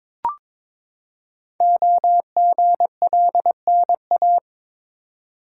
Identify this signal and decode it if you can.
Morse code, heard as a keyed tone, about 22 wpm, 700 Hz. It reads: OGLNA
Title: Code Groups - 5 Letters - 22wpm